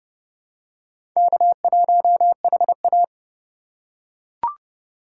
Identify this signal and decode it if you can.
Morse code, heard as a keyed tone, about 29 wpm, 700 Hz. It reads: K1HA
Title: USA Call Signs - 4 Characters - 2x Character Spacing - 30wpm